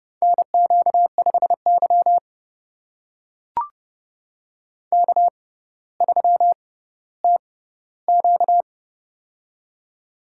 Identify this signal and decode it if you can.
Morse code, heard as a keyed tone, about 30 wpm, spaced as for 15 wpm, 700 Hz. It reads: NQ5Y K3TQ
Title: USA Call Signs - 6x Character Spacing - 30wpm